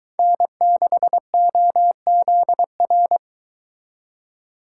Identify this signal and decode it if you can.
Morse code, heard as a keyed tone, about 22 wpm, 700 Hz. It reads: N6OZR